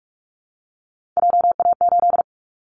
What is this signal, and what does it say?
Morse code, heard as a keyed tone, about 44 wpm, 700 Hz. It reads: JA8
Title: Callsign Prefixes for Top 25 DXCC - Repeated 3x - 45wpm